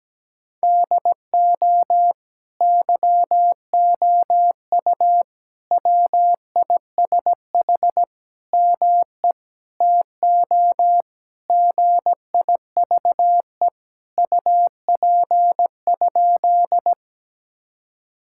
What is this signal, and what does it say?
Morse code, heard as a keyed tone, about 17 wpm, 700 Hz. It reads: DO YOU WISH ME TO GIVE UP?
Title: Sentences from Top 1600 Words 17wpm